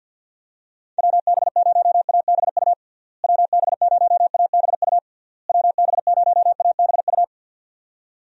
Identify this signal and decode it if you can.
Morse code, heard as a keyed tone, about 52 wpm, 700 Hz. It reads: WB0ABU WB0ABU WB0ABU
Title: USA Call Signs - 6 Characters - Repeated 3x - 50wpm